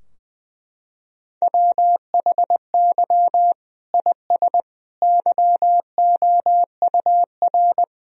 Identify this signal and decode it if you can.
Morse code, heard as a keyed tone, about 20 wpm, 700 Hz. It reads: WHY IS YOUR
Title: Sentences from Top 1000 Words - Review Part 2 - 20wpm